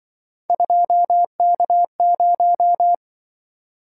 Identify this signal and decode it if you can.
Morse code, heard as a keyed tone, about 26 wpm, 700 Hz. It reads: 2K0